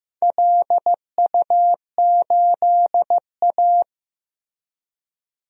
Morse code, heard as a keyed tone, about 15 wpm, 700 Hz. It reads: LU8A
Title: DXCC-100 Call Signs - 4 Characters - Repeated 3x - 15wpm